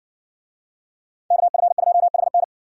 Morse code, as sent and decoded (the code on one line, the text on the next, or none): -.- ..-. ..--- .... -.
KF2HN